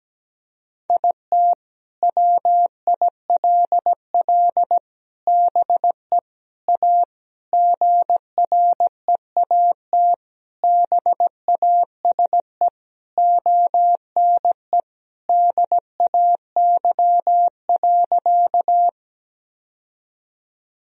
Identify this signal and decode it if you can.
Morse code, heard as a keyed tone, about 17 wpm, 700 Hz. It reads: IT WILL BE A GREAT BASE ONE DAY.
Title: Sentences from Top 1000 Words - Review Part 1 - 17wpm